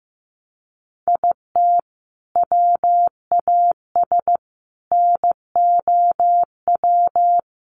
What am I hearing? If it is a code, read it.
Morse code, heard as a keyed tone, about 15 wpm, 700 Hz. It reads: IT WAS NOW
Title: Sentences from Top 200 Words 15wpm